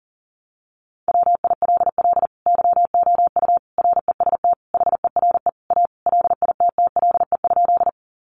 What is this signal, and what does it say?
Morse code, heard as a keyed tone, about 40 wpm, 700 Hz. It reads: WILL YOU REST HERE A LITTLE?
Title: Sentences from Top 500 Words 40wpm